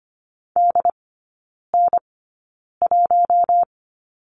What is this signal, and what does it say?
Morse code, heard as a keyed tone, about 25 wpm, spaced as for 9 wpm, 700 Hz. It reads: DN1
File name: Lesson 14e - Number 2 - Three Characters 25wpm